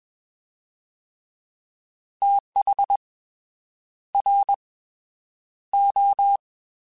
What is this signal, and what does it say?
Morse code, heard as a keyed tone, about 21 wpm, spaced as for 8 wpm, 800 Hz. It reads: THRO
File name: TH / TH mp3s / Fast TH @20